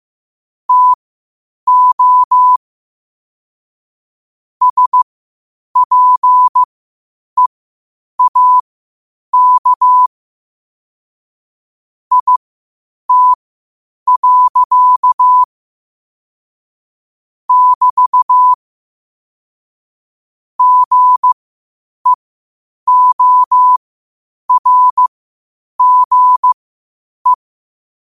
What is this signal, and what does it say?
Morse code, heard as a keyed tone, about 15 wpm, spaced as for 8 wpm, 1000 Hz. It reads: TO SPEAK IT. <BT> GEORGE
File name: qotdfiles / 8wpm